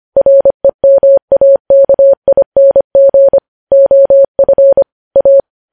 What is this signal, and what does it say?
Morse code, heard as a keyed tone, about 25 wpm, 550 Hz. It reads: REMAKING OF A
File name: podcast / News headlines-1x-25x25